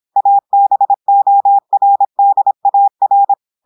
Morse code, heard as a keyed tone, about 26 wpm, 800 Hz. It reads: ABORDAR
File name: portugais texte en clair 2 vitesse 26 mots minutes